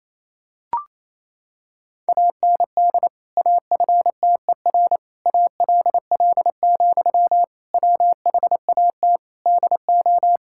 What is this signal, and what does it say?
Morse code, heard as a keyed tone, about 27 wpm, 700 Hz. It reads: AND AFTER ALL, WHAT DO
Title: Sentences from Top 1600 Words 28wpm